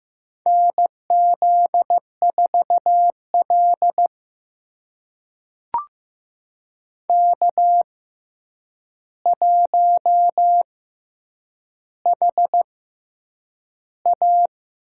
Morse code, heard as a keyed tone, about 15 wpm, spaced as for 8 wpm, 700 Hz. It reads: NZ4L K1HA
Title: USA Call Signs - 4 Characters - 6x Character Spacing - 15wpm